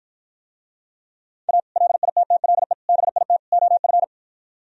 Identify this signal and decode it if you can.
Morse code, heard as a keyed tone, about 52 wpm, 700 Hz. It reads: A LITTLE BIT OF